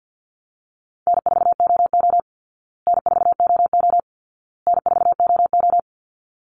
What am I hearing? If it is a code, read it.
Morse code, heard as a keyed tone, about 50 wpm, 700 Hz. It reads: N4OO N4OO N4OO
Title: Easy Call Signs Repeated 3x - 50wpm